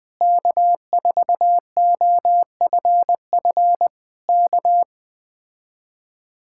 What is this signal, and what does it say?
Morse code, heard as a keyed tone, about 20 wpm, 700 Hz. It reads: K4OFF K